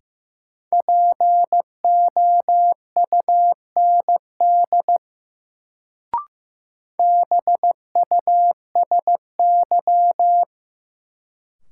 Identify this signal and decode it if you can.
Morse code, heard as a keyed tone, about 15 wpm, 700 Hz. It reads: POUND BUSY
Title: Sentences from Top 1000 Words - Review Part 2 - New Word Drill - 15wpm